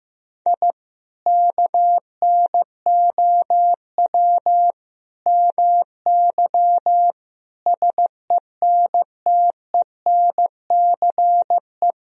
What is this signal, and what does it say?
Morse code, heard as a keyed tone, about 15 wpm, 700 Hz. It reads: I KNOW MY SENTENCE
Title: Sentences from Top 1000 Words - Review Part 1 - 15wpm